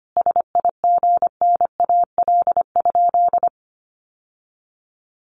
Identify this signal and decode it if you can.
Morse code, heard as a keyed tone, about 25 wpm, 700 Hz. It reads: SIGNAL?
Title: Sentences from Top 1400 Words 25wpm